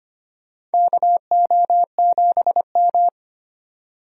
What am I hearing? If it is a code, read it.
Morse code, heard as a keyed tone, about 25 wpm, 700 Hz. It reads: KO7M
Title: USA Call Signs - 4 Characters - 25wpm